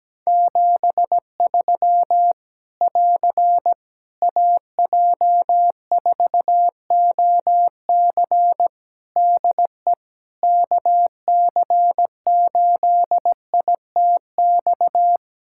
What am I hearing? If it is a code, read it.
Morse code, heard as a keyed tone, about 17 wpm, 700 Hz. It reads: 73 <AR> AJ4OC DE KC8ITX
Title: Practice QSO - Exchange 4 - Ending - 17wpm